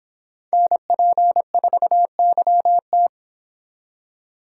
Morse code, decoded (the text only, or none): NP4YT